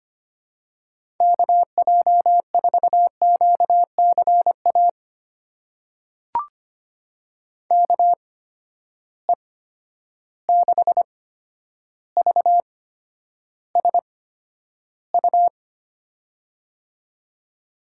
Morse code, heard as a keyed tone, about 25 wpm, spaced as for 11 wpm, 700 Hz. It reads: KJ4QCA KE6VSU
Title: USA Call Signs - 6 Characters - 8x Character Spacing - 25wpm